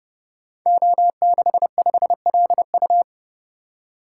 Morse code, heard as a keyed tone, about 30 wpm, 700 Hz. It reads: O65LU
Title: Code Groups - 5 Letter:Numbers - 30wpm